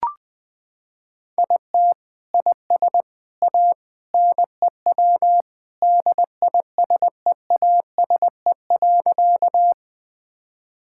Morse code, heard as a keyed tone, about 20 wpm, 700 Hz. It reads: IT IS A NEW DISEASE.
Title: Sentences from Top 1000 Words - Review Part 6 - 20wpm